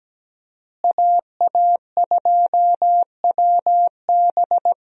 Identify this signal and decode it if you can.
Morse code, heard as a keyed tone, about 17 wpm, 700 Hz. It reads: AA2WB
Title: USA Call Signs - 5 Characters - Repeated 3x - 17wpm